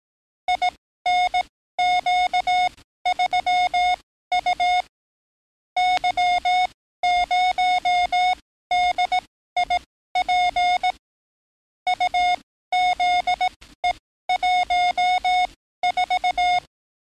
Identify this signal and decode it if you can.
Morse code, heard as a keyed tone, about 18 wpm, spaced as for 13 wpm, 720 Hz. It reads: INQ3U Y0DIP UZE14
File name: training / NAVPERS 11049to11060 / NAVPERS 11056 RA RB